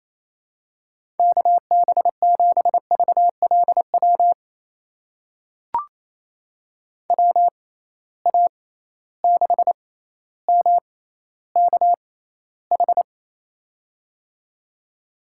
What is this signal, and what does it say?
Morse code, heard as a keyed tone, about 28 wpm, spaced as for 14 wpm, 700 Hz. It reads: KB7VLW WA6MKH
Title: USA Call Signs - 6 Characters - 6x Character Spacing - 28wpm